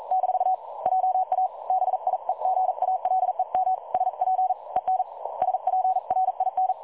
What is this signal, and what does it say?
Morse code, heard as a keyed tone, about 39 wpm, 750 Hz. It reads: - JA BIEGALEM SP EN SPRIN
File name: dlazaawansowanych / nagraniazpasma / websdr recording 3565.0kHz